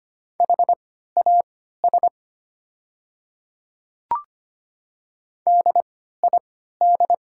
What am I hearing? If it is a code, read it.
Morse code, heard as a keyed tone, about 25 wpm, spaced as for 14 wpm, 700 Hz. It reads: HAS DID
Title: 3x Character Spacing - Top 100 Words - 25wpm